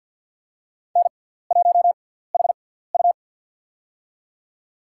Morse code, heard as a keyed tone, about 53 wpm, spaced as for 17 wpm, 690 Hz. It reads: N1HU
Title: USA Call Signs - 4 Characters - 6x Character Spacing - 50wpm